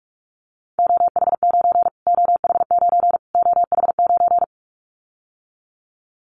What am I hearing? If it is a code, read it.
Morse code, heard as a keyed tone, about 44 wpm, 700 Hz. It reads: OH9 OH9 OH9
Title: Callsign Prefixes for Top 25 DXCC - Repeated 3x - 45wpm